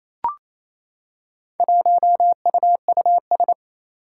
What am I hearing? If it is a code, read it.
Morse code, heard as a keyed tone, about 27 wpm, 700 Hz. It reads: 1UUS
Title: Four Challenging Letter-Number Combinations 28wpm